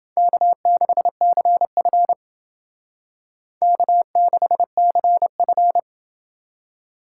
Morse code, heard as a keyed tone, about 31 wpm, 700 Hz. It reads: K6CF K6CF